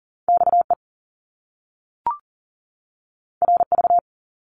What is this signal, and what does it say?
Morse code, heard as a keyed tone, about 38 wpm, 700 Hz. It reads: XE RV
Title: Top Two Letter Combinations in Words 40wpm